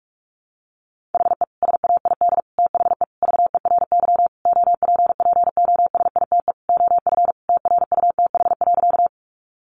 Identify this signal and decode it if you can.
Morse code, heard as a keyed tone, about 44 wpm, 700 Hz. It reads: HE SAID THE VERY OPPOSITE OF TRUTH.
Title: Sentences from Top 1300 Words 45wpm